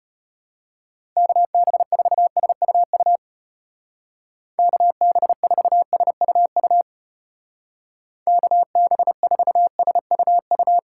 Morse code, heard as a keyed tone, about 35 wpm, 700 Hz. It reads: KB4SUU KB4SUU KB4SUU